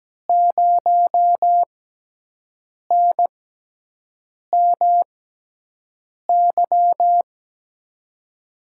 Morse code, heard as a keyed tone, about 17 wpm, spaced as for 6 wpm, 700 Hz. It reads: 0NMY
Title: USA Call Signs - 6x Character Spacing - 17wpm